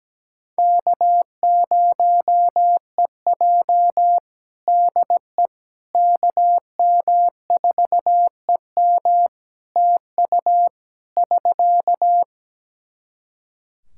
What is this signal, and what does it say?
Morse code, heard as a keyed tone, about 17 wpm, 700 Hz. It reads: K0EJ DE KM4EM TU <SK>
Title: Practice QSO - Exchange 4 - Ending - 17wpm